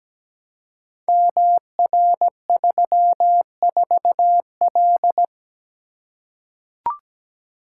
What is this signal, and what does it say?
Morse code, heard as a keyed tone, about 17 wpm, 700 Hz. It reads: MR34L
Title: Code Groups - 5 Letter:Numbers - 17wpm